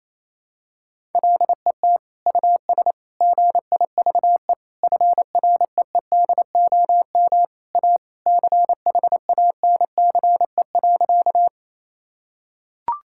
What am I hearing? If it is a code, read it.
Morse code, heard as a keyed tone, about 28 wpm, 700 Hz. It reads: LET US GIVE FREEDOM A CHANCE.
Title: Sentences from Top 1000 Words - Review Part 5 - 28wpm